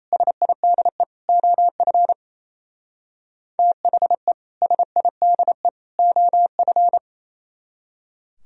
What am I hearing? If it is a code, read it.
Morse code, heard as a keyed tone, about 35 wpm, 700 Hz. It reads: SIDE OF THE SIDE OF